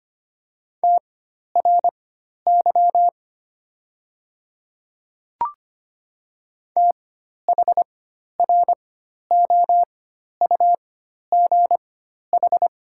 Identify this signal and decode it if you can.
Morse code, heard as a keyed tone, about 25 wpm, spaced as for 12 wpm, 690 Hz. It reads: TRY THROUGH
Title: 4x Character Spacing - Top 300 Words - 25wpm